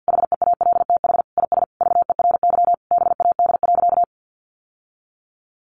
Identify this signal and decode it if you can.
Morse code, heard as a keyed tone, about 50 wpm, 700 Hz. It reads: HEALTH IS VERY BAD.
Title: Sentences from Top 1200 Words 50wpm